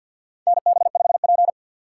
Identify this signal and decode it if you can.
Morse code, heard as a keyed tone, about 53 wpm, 700 Hz. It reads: NB5P
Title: USA Call Signs - 6x Character Spacing - 50wpm